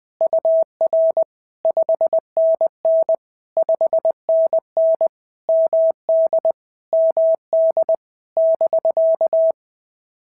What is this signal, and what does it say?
Morse code, heard as a keyed tone, about 20 wpm, 650 Hz. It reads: UR 5NN 5NN MD MD <BK>
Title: POTA QSO Practice - 20wpm